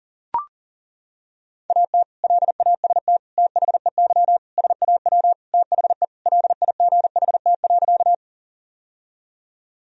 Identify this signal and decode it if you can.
Morse code, heard as a keyed tone, about 40 wpm, 700 Hz. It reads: AT LAST THEY SAW THE LIGHT.
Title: Sentences from Top 1000 Words - Review Part 1 - 40wpm